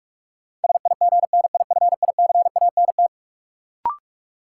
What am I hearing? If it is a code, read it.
Morse code, heard as a keyed tone, about 46 wpm, 700 Hz. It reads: SIGNIFICANT
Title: Sentences from Top 1000 Words - Review Part 6 - New Word Drill - 45wpm